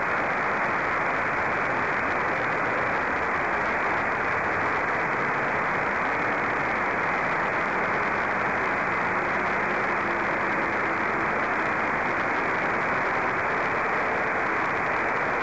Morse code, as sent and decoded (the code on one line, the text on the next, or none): none